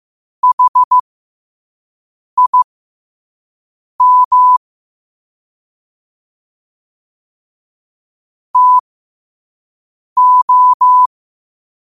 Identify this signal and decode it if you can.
Morse code, heard as a keyed tone, about 15 wpm, spaced as for 5 wpm, 1000 Hz. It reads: HIM TO